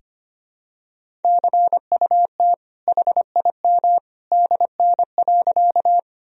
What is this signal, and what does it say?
Morse code, heard as a keyed tone, about 25 wpm, 700 Hz. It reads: CUT HIM DN.
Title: Sentences from Top 300 Words and Abbreviations 25wpm